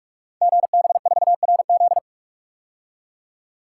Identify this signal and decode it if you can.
Morse code, heard as a keyed tone, about 44 wpm, 700 Hz. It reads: GB4RZ
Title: Code Groups - 5 Letter:Numbers - 45wpm